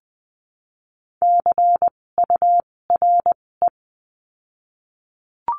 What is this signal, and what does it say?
Morse code, heard as a keyed tone, about 20 wpm, spaced as for 16 wpm, 700 Hz. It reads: CURE
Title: 256 Lesson 42a - Words 12-20wpm